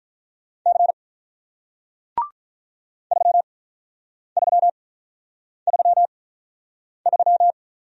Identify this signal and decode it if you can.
Morse code, heard as a keyed tone, about 48 wpm, spaced as for 9 wpm, 700 Hz. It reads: C 3333